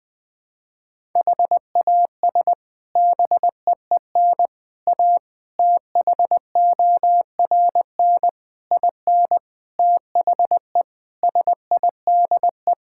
Morse code, heard as a keyed tone, about 20 wpm, 700 Hz. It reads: HAS BEEN A THORN IN THE SIDE
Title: Sets of 8 Words - 20wpm